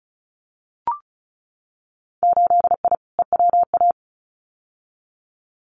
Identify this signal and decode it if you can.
Morse code, heard as a keyed tone, about 34 wpm, 700 Hz. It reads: 8I EWA
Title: Winter Field Day Exchanges Encore 35wpm